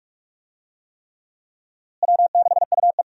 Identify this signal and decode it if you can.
Morse code, heard as a keyed tone, about 46 wpm, 700 Hz. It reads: W6UE
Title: USA Call Signs - 4 Characters - Repeated 3x - 45wpm